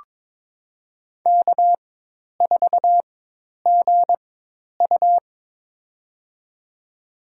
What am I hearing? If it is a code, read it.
Morse code, heard as a keyed tone, about 22 wpm, spaced as for 10 wpm, 700 Hz. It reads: K4GU
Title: USA Call Signs - 4x Character Spacing - 22wpm